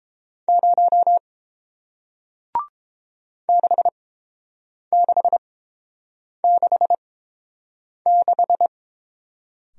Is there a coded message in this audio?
Yes: Morse code from 0.5 to 8.7 s, heard as a keyed tone, about 28 wpm, spaced as for 7 wpm, 690 Hz. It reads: 0 6666